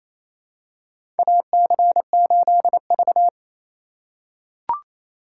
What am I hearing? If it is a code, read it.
Morse code, heard as a keyed tone, about 28 wpm, 700 Hz. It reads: AC8V
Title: USA Call Signs - 4 Characters - 2x Character Spacing - 28wpm